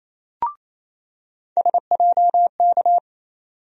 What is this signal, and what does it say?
Morse code, heard as a keyed tone, about 28 wpm, 700 Hz. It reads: SJK
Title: USA Call Sign Postfixes - 3 Letters 28wpm